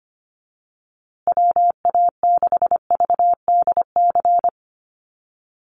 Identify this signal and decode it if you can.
Morse code, heard as a keyed tone, about 25 wpm, 700 Hz. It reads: WA6VDC
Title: USA Call Signs Repeated 2x 25wpm